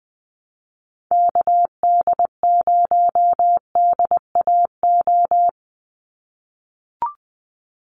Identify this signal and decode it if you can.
Morse code, heard as a keyed tone, about 20 wpm, 700 Hz. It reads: KD0DAO
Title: USA Call Signs 20wpm